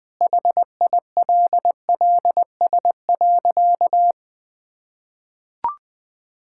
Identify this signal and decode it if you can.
Morse code, heard as a keyed tone, about 20 wpm, 700 Hz. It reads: HILLS.
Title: Sentences with 2-Word Phrases - 20wpm